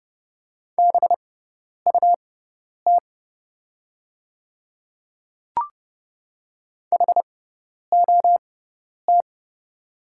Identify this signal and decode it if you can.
Morse code, heard as a keyed tone, about 29 wpm, spaced as for 10 wpm, 690 Hz. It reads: BUT HOT
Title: 6x Character Spacing - Top 200 Words - 30wpm